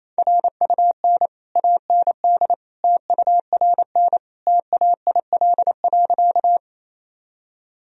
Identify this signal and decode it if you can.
Morse code, heard as a keyed tone, about 28 wpm, 700 Hz. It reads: RUN AND TURN TAIL.
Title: Sentences from Top 1000 Words - Review Part 3 - 28wpm